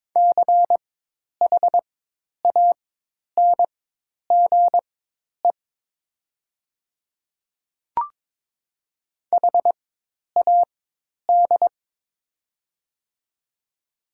Morse code, heard as a keyed tone, about 22 wpm, spaced as for 10 wpm, 700 Hz. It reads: CHANGE HAD
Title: 4x Character Spacing - Top 300 Words - 22wpm